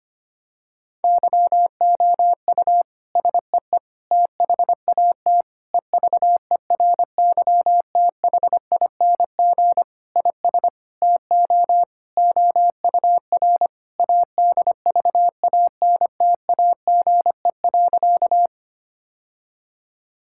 Morse code, heard as a keyed tone, about 25 wpm, 700 Hz. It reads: YOU SEE THAT EVERYTHING IS TO OUR ADVANTAGE.
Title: Sentences from Top 1000 Words - Review Part 6 - 25wpm